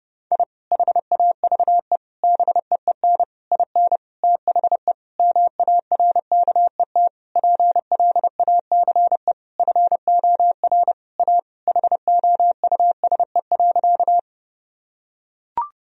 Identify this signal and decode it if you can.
Morse code, heard as a keyed tone, about 29 wpm, 700 Hz. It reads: I HAVE BEEN IN THE MARKET PLACE FOR A HOUSE.
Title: Sentences from Top 1000 Words - Review Part 3 - 30wpm